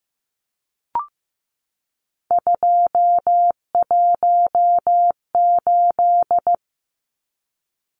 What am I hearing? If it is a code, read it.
Morse code, heard as a keyed tone, about 15 wpm, 700 Hz. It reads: 218
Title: Numbers - 3-Digit 15wpm